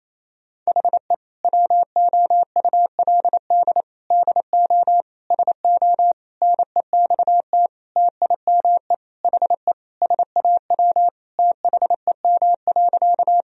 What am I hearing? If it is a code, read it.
Morse code, heard as a keyed tone, about 28 wpm, 700 Hz. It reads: HE WOULD DO SO NEXT TIME HE SAW THEM.
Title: Sentences from Top 1000 Words - Review Part 1 - 28wpm